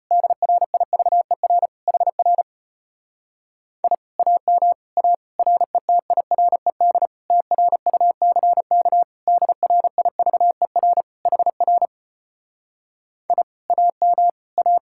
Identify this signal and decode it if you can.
Morse code, heard as a keyed tone, about 33 wpm, 700 Hz. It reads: DRIVER HR I AM A RETIRED TRUCK DRIVER HR I AM A